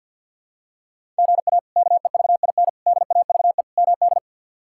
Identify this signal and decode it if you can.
Morse code, heard as a keyed tone, about 52 wpm, 700 Hz. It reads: GA KEVIN DAVE MD
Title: K1USN SST QSO Practice - 50wpm